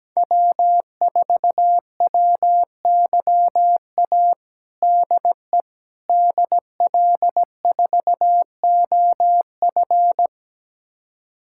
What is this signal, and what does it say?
Morse code, heard as a keyed tone, about 17 wpm, 700 Hz. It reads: W4WYA DE DL4OF
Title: USA Callsign DE Callsign - 5 Characters - 17wpm